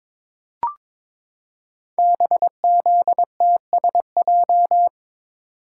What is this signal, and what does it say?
Morse code, heard as a keyed tone, about 22 wpm, 700 Hz. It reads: BZTSJ
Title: Code Groups - 5 Letters - 22wpm